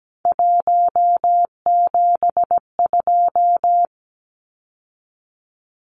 Morse code, heard as a keyed tone, about 17 wpm, 700 Hz. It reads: 172
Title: Numbers - 3-Digit 17wpm